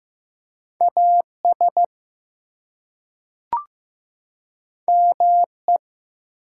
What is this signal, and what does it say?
Morse code, heard as a keyed tone, about 15 wpm, 700 Hz. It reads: AS ME
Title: Two Letter Words - 15wpm